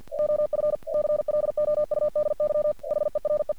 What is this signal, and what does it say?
Morse code, heard as a keyed tone, about 48 wpm, 600 Hz. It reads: OF CLOUDY HERE
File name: archived / CRSnet